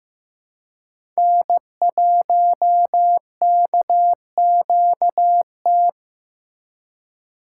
Morse code, heard as a keyed tone, about 15 wpm, 700 Hz. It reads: N1KQT
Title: USA Call Signs - 5 Characters - 2x Character Spacing - 15wpm